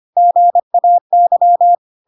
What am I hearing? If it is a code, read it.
Morse code, heard as a keyed tone, about 25 wpm, 700 Hz. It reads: GAY
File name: Pinocchio / Pinocchio ch1 25 25 ik1hge